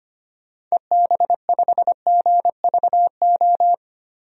Morse code, heard as a keyed tone, about 25 wpm, 700 Hz. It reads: EB5GVO
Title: DXCC-100 Call Signs - 6 Characters - Repeated 3x - 25wpm